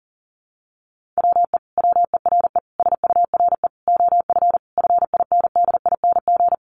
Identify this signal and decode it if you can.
Morse code, heard as a keyed tone, about 40 wpm, 700 Hz. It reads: WE WERE SURE OF FINDING
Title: Sentences from Top 1100 Words 40wpm